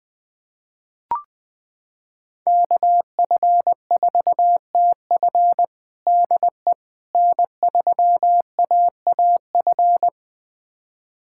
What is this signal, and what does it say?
Morse code, heard as a keyed tone, about 21 wpm, 700 Hz. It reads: KF4TF DE N3AAF
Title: USA Callsign DE Callsign - 5 Characters - 20wpm